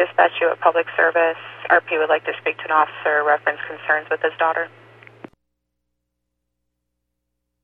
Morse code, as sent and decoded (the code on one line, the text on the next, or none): none